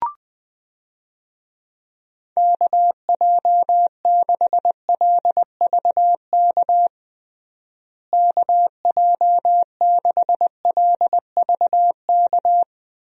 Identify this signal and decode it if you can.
Morse code, heard as a keyed tone, about 21 wpm, 700 Hz. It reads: KJ6LVK KJ6LVK
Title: USA Call Signs - 6 Characters - Repeated 3x - 20wpm